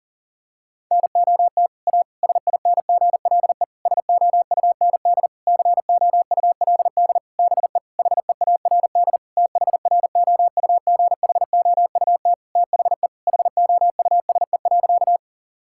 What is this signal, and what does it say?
Morse code, heard as a keyed tone, about 37 wpm, 700 Hz. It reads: NOT A SINGLE SOUND COULD BE HEARD THROUGHOUT THE HOUSE.
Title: Sentences from Top 1000 Words - Review Part 5 - 40wpm